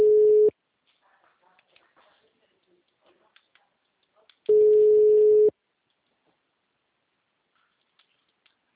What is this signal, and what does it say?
Ringback tone (Europe, 425 Hz).